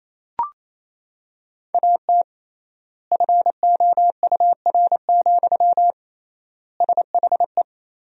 Morse code, heard as a keyed tone, about 28 wpm, 700 Hz. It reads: AT FOUR, SHE
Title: 3x Word-Spacing - Sentences from Top 300 Words - 28wpm